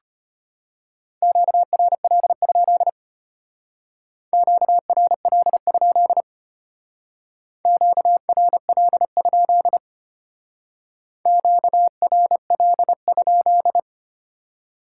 Morse code, heard as a keyed tone, about 33 wpm, 700 Hz. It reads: QRL? QRL? QRL? QRL?